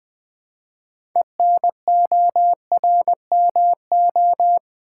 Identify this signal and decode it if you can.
Morse code, heard as a keyed tone, about 20 wpm, 700 Hz. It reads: ENORMO
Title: Build A Word - 3 Letters At A Time - 8-Letter Words - 20wpm